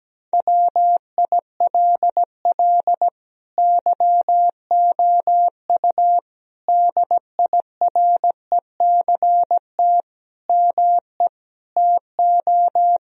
Morse code, heard as a keyed tone, about 17 wpm, 700 Hz. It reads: WILL YOU DIRECT ME TO
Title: Sentences from Top 1000 Words - Review Part 2 - 17wpm